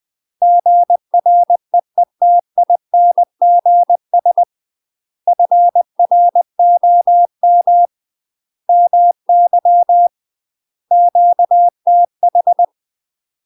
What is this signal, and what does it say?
Morse code, heard as a keyed tone, about 20 wpm, 700 Hz. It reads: GREETINGS FROM MY QTH
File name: UK MASTER QSOs TRAINING / 20WPM / UK MASTER QSOs 0048